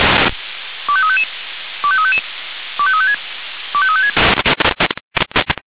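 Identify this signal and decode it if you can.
Five-tone selective call (ZVEI), heard as quick runs of five tones, tones of about 70 ms. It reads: 25355 25355 25356 25356
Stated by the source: ZVEI five-tone selective call 25354 received by SDR.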